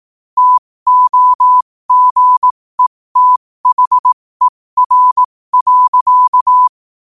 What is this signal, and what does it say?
Morse code, heard as a keyed tone, about 18 wpm, spaced as for 15 wpm, 1000 Hz. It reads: TOGETHER.
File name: 15wpm